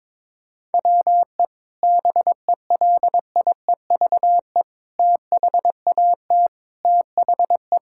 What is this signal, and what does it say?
Morse code, heard as a keyed tone, about 22 wpm, 700 Hz. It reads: WE BELIEVE THAT THE
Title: Sentences from Top 1000 Words - Review Part 6 - 22wpm